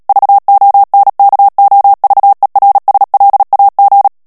Morse code, heard as a keyed tone, about 37 wpm, 800 Hz. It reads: VONKOVERSLAG
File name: morse mp3 / HOLLANDAIS texte clair 2 vitesse 35 mots minute